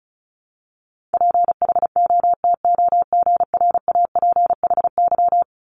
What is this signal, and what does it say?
Morse code, heard as a keyed tone, about 36 wpm, 700 Hz. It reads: PHOTOGRAPHY
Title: Multi-letter Phonemes 35wpm